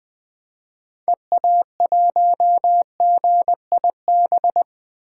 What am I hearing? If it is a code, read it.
Morse code, heard as a keyed tone, about 20 wpm, 700 Hz. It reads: EA1GIB
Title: 2023 POTA Hunter Call Signs - 20wpm